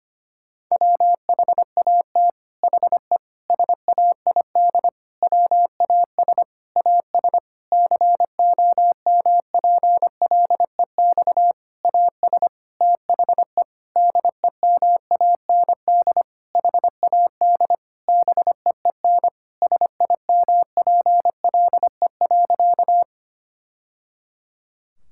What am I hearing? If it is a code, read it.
Morse code, heard as a keyed tone, about 25 wpm, 700 Hz. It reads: WHAT HE SAID WAS AS COMPLEX AS THE DEMAND HAD BEEN SIMPLE.
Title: Sentences from Top 1000 Words - Review Part 6 - 25wpm